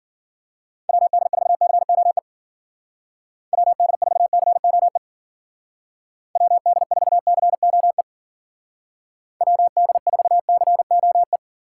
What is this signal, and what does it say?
Morse code, heard as a keyed tone, about 50 wpm, 700 Hz. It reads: WD4COE WD4COE WD4COE WD4COE
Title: Speed Racing - USA Call Signs 40wpm